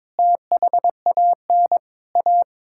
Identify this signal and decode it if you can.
Morse code, heard as a keyed tone, about 22 wpm, 700 Hz. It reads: THAN A
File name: practice / 22wpm / Sets of 8 Words - 22wpm